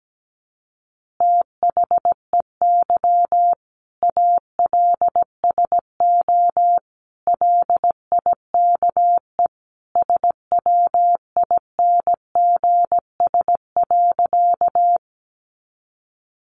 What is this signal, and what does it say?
Morse code, heard as a keyed tone, about 17 wpm, 700 Hz. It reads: THEY ALSO LIKE SWINGS.
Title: Easy Phrases 17wpm